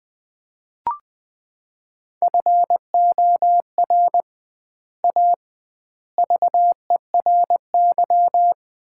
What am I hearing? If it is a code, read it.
Morse code, heard as a keyed tone, about 20 wpm, 700 Hz. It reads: FOR A VERY
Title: 2x Word-Spacing - Sets of 5 Words - 20wpm